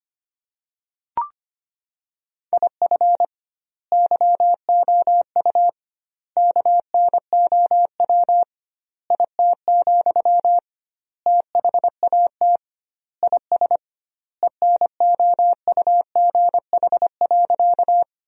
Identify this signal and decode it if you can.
Morse code, heard as a keyed tone, about 25 wpm, 700 Hz. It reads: IF YOU KNOW IT, THAT IS ENOUGH.
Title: 2x Word-Spacing - Sentences from Top 400 Words - 25wpm